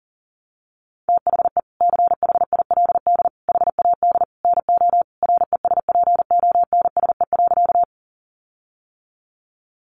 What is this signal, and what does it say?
Morse code, heard as a keyed tone, about 40 wpm, 700 Hz. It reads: THE CHILD HAD NO RESPONSE.
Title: Sentences from Top 1400 Words 40wpm